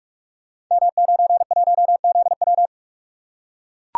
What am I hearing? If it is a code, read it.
Morse code, heard as a keyed tone, about 44 wpm, 700 Hz. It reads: M91ZW
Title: Code Groups - 5 Letter:Numbers - 45wpm